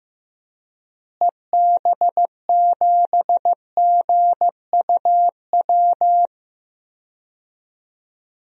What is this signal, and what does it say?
Morse code, heard as a keyed tone, about 15 wpm, 700 Hz. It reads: EB7GUW
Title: DXCC-100 Call Signs - 6 Characters - Repeated 2x - 15wpm